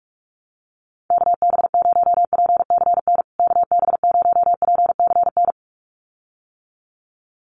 Morse code, heard as a keyed tone, about 44 wpm, 700 Hz. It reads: KB0PCN KB0PCN
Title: USA Call Signs Repeated 2x 45wpm